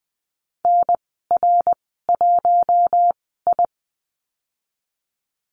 Morse code, heard as a keyed tone, about 20 wpm, spaced as for 15 wpm, 700 Hz. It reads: NR1I
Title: 147 Lesson 24g - Letter y - Call Signs 20wpm